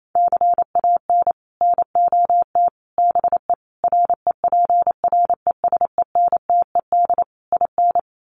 Morse code, heard as a keyed tone, about 28 wpm, 700 Hz. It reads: CAN NOT BE REPRESENTED IN
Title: Sets of 5 Words 28wpm